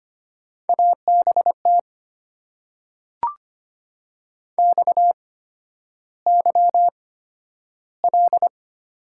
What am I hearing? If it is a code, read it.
Morse code, heard as a keyed tone, about 25 wpm, spaced as for 11 wpm, 700 Hz. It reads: ABT XYL
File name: Top 90 QSO Elements - 8x Character Spacing - 25wpm